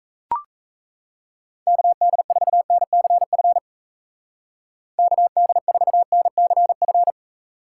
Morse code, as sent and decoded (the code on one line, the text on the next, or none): -.- -.. ....- -. -.-. ..-. / -.- -.. ....- -. -.-. ..-.
KD4NCF KD4NCF